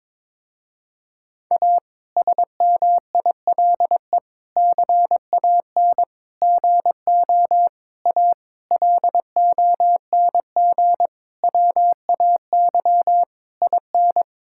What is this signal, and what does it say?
Morse code, heard as a keyed tone, about 22 wpm, 700 Hz. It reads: A SMILE CAN GO A LONG WAY IN
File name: Difficult Sentences with Hints - 22wpm